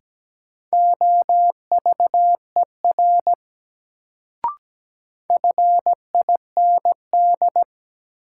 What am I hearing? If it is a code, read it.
Morse code, heard as a keyed tone, about 17 wpm, 700 Hz. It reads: OVER FIND
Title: Sets of 2 Words - New Word Drill - 17wpm